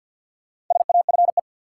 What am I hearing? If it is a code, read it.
Morse code, heard as a keyed tone, about 46 wpm, 700 Hz. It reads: SAFE
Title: Sentences from Top 900 Words - New Word Drill - 50wpm